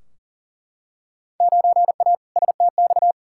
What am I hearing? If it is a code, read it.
Morse code, heard as a keyed tone, about 40 wpm, 700 Hz. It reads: 9A STX
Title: Field Day Exchanges - 40wpm